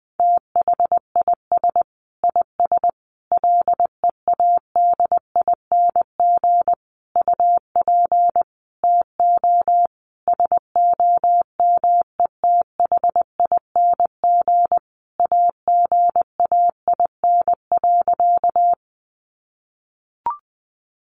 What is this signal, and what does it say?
Morse code, heard as a keyed tone, about 20 wpm, 700 Hz. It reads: THIS IS LEADING UP TO SOMETHING AGAIN.
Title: Sentences from Top 1100 Words 20wpm